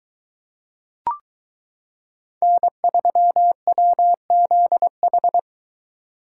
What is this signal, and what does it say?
Morse code, heard as a keyed tone, about 22 wpm, 700 Hz. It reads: N3WZH